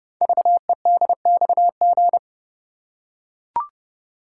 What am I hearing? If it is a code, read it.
Morse code, heard as a keyed tone, about 30 wpm, 700 Hz. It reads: VEDXG
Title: Code Groups - 5 Letters - 30wpm